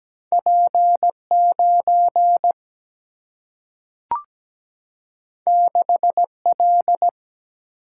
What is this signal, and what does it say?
Morse code, heard as a keyed tone, about 17 wpm, 700 Hz. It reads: P 9 6 L
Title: Single-Letter Single-Number 17wpm